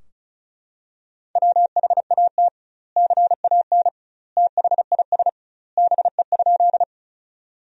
Morse code, heard as a keyed tone, about 34 wpm, 700 Hz. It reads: WHAT CAN THIS BE?
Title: 2x Word-Spacing - Sentences from Top 100 Words - 35wpm